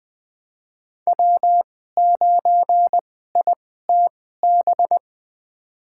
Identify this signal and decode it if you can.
Morse code, heard as a keyed tone, about 20 wpm, spaced as for 14 wpm, 700 Hz. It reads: W9ITB
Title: USA Call Signs - 2x Character Spacing - 20wpm